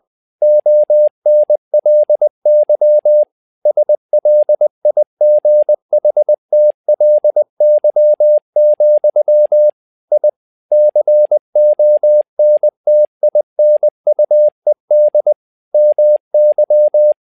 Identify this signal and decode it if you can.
Morse code, heard as a keyed tone, about 20 wpm, 600 Hz. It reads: ONLY SLIGHTLY, I CONTINUED MY
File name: learning center / tower trouble / tower trouble 20wpm